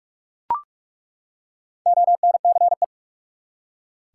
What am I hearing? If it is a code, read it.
Morse code, heard as a keyed tone, about 41 wpm, 700 Hz. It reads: ONCE
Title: Four Letter Words - 45wpm